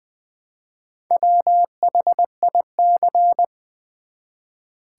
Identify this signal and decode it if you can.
Morse code, heard as a keyed tone, about 20 wpm, 700 Hz. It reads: WHIC
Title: Build A Word - 1 Letter At A Time - 3 to 5-Letter Words - 20wpm